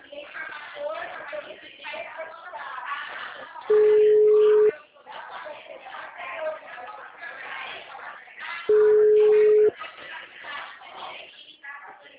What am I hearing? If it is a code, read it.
Ringback tone (Europe, 425 Hz), with speech.
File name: out- -2010- - - .89448